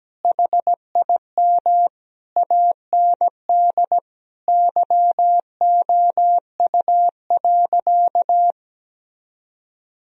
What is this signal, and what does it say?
Morse code, heard as a keyed tone, about 17 wpm, 700 Hz. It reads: HIM AND YOU.
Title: Sentences from Top 1000 Words - Review Part 1 - 17wpm